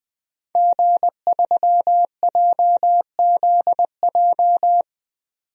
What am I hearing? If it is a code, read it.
Morse code, heard as a keyed tone, about 20 wpm, 700 Hz. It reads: G3JZJ
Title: DXCC-100 Call Signs - 5 Characters - 20wpm